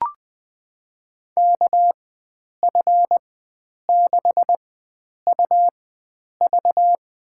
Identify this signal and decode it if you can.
Morse code, heard as a keyed tone, about 20 wpm, spaced as for 9 wpm, 700 Hz. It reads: KF6UV
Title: USA Call Signs - 4x Character Spacing - 20wpm